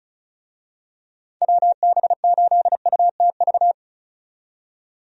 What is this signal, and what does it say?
Morse code, heard as a keyed tone, about 34 wpm, 700 Hz. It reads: WB8UTV
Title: USA Call Signs - 6 Characters - 35wpm